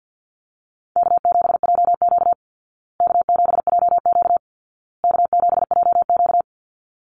Morse code, heard as a keyed tone, about 50 wpm, 700 Hz. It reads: K7JQ K7JQ K7JQ
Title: Easy Call Signs Repeated 3x - 50wpm